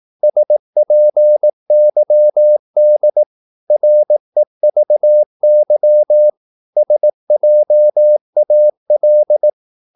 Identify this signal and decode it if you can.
Morse code, heard as a keyed tone, about 18 wpm, 600 Hz. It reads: SPYD REVY SJAL
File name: Korte ord DK 18wpm